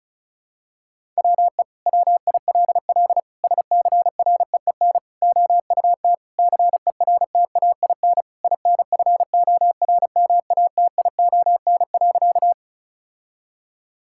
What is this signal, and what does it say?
Morse code, heard as a keyed tone, about 36 wpm, 700 Hz. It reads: WE WILL SCREEN OUT CERTAIN INFORMATION.
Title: Sentences from Top 1000 Words - Review Part 6 - 35wpm